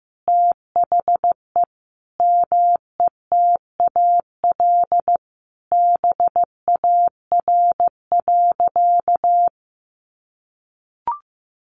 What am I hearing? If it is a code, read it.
Morse code, heard as a keyed tone, about 14 wpm, 700 Hz. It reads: THE METAL BAR.
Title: Additional Easy Phrases 15wpm